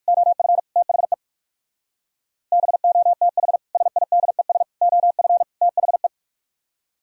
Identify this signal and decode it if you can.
Morse code, heard as a keyed tone, about 61 wpm, spaced as for 53 wpm, 700 Hz. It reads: OF THE BOTH SIDES OF THE
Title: Speed-Racing - Sets of 4 Words 45wpm